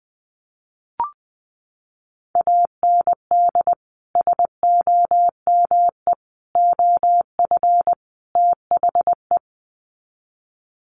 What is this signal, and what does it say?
Morse code, heard as a keyed tone, about 20 wpm, 700 Hz. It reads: AND SOME OF THE
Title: Sets of 4 Words 20wpm